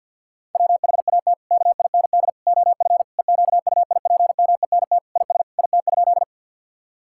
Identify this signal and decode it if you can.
Morse code, heard as a keyed tone, about 50 wpm, 700 Hz. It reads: WHAT KIND OF EQUIPMENT IS IT?
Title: Sentences from Top 1000 Words - Review Part 5 - 50wpm